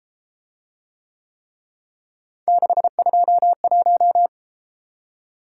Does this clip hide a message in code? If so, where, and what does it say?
Yes: Morse code from 2.5 to 4.3 s, heard as a keyed tone, about 32 wpm, 700 Hz. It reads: 621